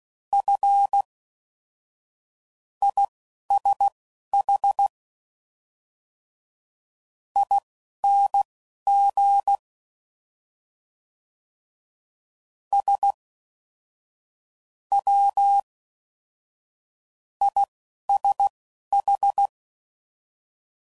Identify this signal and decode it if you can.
Morse code, heard as a keyed tone, about 16 wpm, spaced as for 11 wpm, 800 Hz. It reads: F ISH ING S W ISH